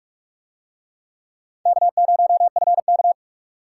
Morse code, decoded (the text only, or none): K0FK